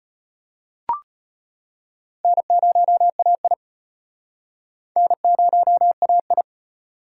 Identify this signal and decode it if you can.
Morse code, heard as a keyed tone, about 33 wpm, 700 Hz. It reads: N0AI N0AI